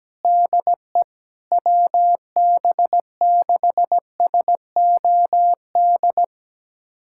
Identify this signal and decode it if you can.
Morse code, heard as a keyed tone, about 17 wpm, 700 Hz. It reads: DE WB6SOD
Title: USA Callsign DE Callsign - 6 Characters - 17wpm